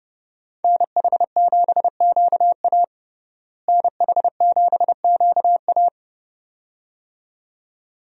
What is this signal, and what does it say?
Morse code, heard as a keyed tone, about 29 wpm, 700 Hz. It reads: NH7QA NH7QA
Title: USA Call Signs - 5 Characters - Repeated 2x - 30wpm